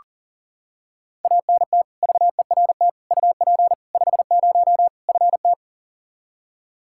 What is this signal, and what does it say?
Morse code, heard as a keyed tone, about 40 wpm, 700 Hz. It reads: ANT VERT UP 50 FT
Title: Practice QSO - Exchange 2 - Info - 40wpm